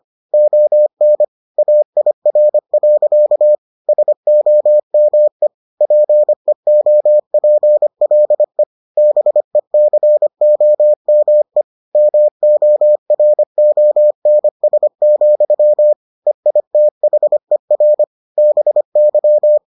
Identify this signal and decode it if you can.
Morse code, heard as a keyed tone, about 25 wpm, 600 Hz. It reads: ON AIR. SOME PEOPLE BECOME MORONS, EITHER BY